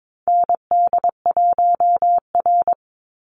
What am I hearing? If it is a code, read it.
Morse code, heard as a keyed tone, about 22 wpm, 700 Hz. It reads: ND1R
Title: Call Signs Repeated 2x - 22wpm